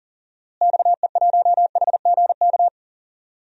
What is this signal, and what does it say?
Morse code, heard as a keyed tone, about 40 wpm, 700 Hz. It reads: XE1HGK
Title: DXCC-100 Call Signs - 6 Characters - Repeated 3x - 40wpm